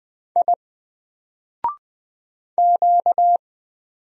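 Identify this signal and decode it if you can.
Morse code, heard as a keyed tone, about 20 wpm, 700 Hz. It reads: I Q
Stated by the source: Single letter and number Morse drill at 20 wpm, from N4LCD.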